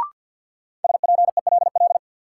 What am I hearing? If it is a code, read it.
Morse code, heard as a keyed tone, about 53 wpm, 700 Hz. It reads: SPELL
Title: Sentences from Top 1000 Words - Review Part 1 - New Word Drill - 50wpm